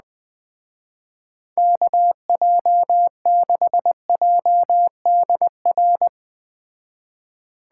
Morse code, heard as a keyed tone, about 20 wpm, 700 Hz. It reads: KJ6JDR
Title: Speed Racing - USA 2x3 Call Signs 17wpm